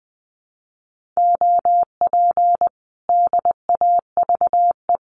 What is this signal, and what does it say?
Morse code, heard as a keyed tone, about 20 wpm, 700 Hz. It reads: OP DAVE
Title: Common QSO Phrases Encore 20wpm